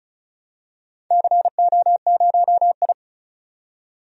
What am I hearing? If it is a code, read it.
Morse code, heard as a keyed tone, about 36 wpm, 700 Hz. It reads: CO0I
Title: DXCC-100 Call Signs - 4 Characters - 35wpm